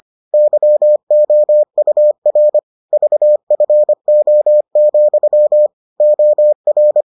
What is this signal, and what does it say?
Morse code, heard as a keyed tone, about 25 wpm, 600 Hz. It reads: YOUR VFO, OR